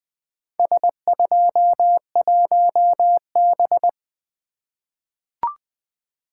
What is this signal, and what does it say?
Morse code, heard as a keyed tone, about 20 wpm, 700 Hz. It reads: S21B
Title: Four Letter-Number Combinations - 20wpm